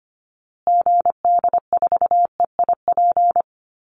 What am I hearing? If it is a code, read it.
Morse code, heard as a keyed tone, about 25 wpm, 700 Hz. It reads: GD4EIP
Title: Call Signs Repeated 3x - 25wpm